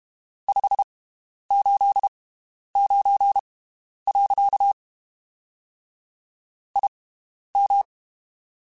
Morse code, heard as a keyed tone, about 32 wpm, spaced as for 11 wpm, 800 Hz. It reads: 589. IM
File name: (spaced) / QSOs Spaced-mp3s / WD0FBT 1 @30 sp